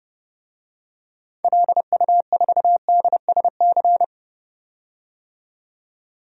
Morse code, heard as a keyed tone, about 30 wpm, 700 Hz. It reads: LU4DSC
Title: DXCC-100 Call Signs - 6 Characters - 30wpm